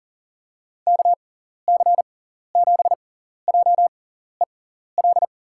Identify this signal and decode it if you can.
Morse code, heard as a keyed tone, about 37 wpm, spaced as for 14 wpm, 700 Hz. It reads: KC7JEL
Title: USA Call Signs - 6 Characters - 6x Character Spacing - 40wpm